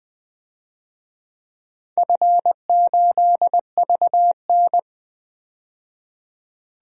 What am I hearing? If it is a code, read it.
Morse code, heard as a keyed tone, about 20 wpm, 700 Hz. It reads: F8VN
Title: DXCC-100 Call Signs - 4 Characters - 20wpm